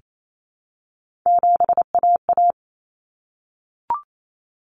Morse code, heard as a keyed tone, about 28 wpm, 700 Hz. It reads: 7AA
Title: Common Cut Numbers - 3-Digit 28wpm